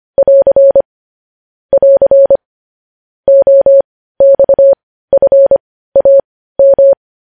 Morse code, heard as a keyed tone, about 25 wpm, spaced as for 15 wpm, 550 Hz. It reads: <AR> <AR> OXFAM
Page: News Headlines in Morse Code at 15 WPM